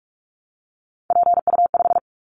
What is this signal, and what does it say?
Morse code, heard as a keyed tone, about 44 wpm, 700 Hz. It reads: PU5
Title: Callsign Prefixes for Top 100 DXCC - 45wpm